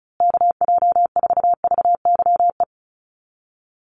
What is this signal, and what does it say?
Morse code, heard as a keyed tone, about 34 wpm, 700 Hz. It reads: KJ4VYE
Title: USA Call Signs Repeated 2x 35wpm